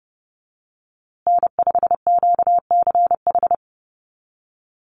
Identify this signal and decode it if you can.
Morse code, heard as a keyed tone, about 29 wpm, 700 Hz. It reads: N5QCH
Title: USA Call Signs Repeated 3x 30wpm